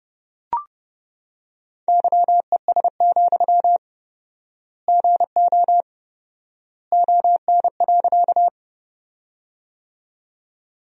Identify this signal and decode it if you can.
Morse code, heard as a keyed tone, about 30 wpm, 700 Hz. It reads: YES, GO ON.
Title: 4x Word-Spacing - Sentences from Top 500 Words - 30wpm